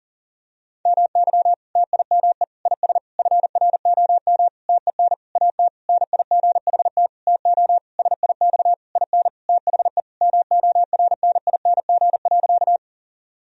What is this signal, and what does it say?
Morse code, heard as a keyed tone, about 40 wpm, 700 Hz. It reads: MY TIME IS FROM TEN AT NIGHT TO SIX IN THE MORNING.
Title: Sentences from Top 1000 Words - Review Part 2 - 40wpm